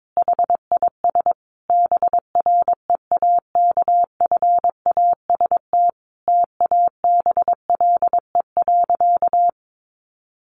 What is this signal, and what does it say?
Morse code, heard as a keyed tone, about 22 wpm, 700 Hz. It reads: HIS BREAKFAST TABLE.
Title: Sentences from Top 1500 Words 22wpm